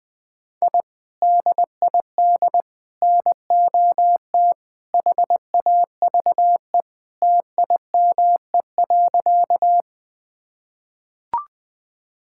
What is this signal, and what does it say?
Morse code, heard as a keyed tone, about 20 wpm, 700 Hz. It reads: I DID NOT HAVE TIME.
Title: Sentences with 3-Word Phrases - 20wpm